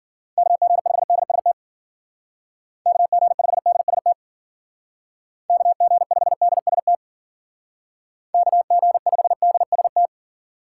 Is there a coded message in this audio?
Yes: Morse code from 0.4 to 10.1 s, heard as a keyed tone, about 50 wpm, 700 Hz. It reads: KG5DST KG5DST KG5DST KG5DST